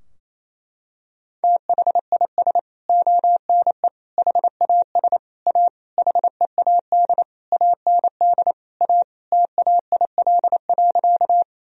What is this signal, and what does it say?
Morse code, heard as a keyed tone, about 27 wpm, 700 Hz. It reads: THIS ONE HAS A HEAD AND A TAIL.
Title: Sentences from Top 1000 Words - Review Part 3 - 28wpm